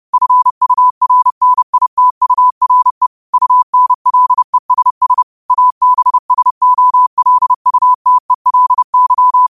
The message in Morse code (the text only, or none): FURNITURE UNLESS ABSOLUTELY